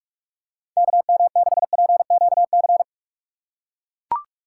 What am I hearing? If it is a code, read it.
Morse code, heard as a keyed tone, about 44 wpm, 700 Hz. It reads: K M 6 P Q C
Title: USA Call Signs - 6 Characters - 6x Character Spacing - 45wpm